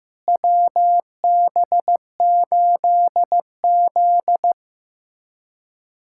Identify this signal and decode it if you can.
Morse code, heard as a keyed tone, about 16 wpm, 700 Hz. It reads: WB8Z
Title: USA Callsign DE Callsign - 4 Characters - 15wpm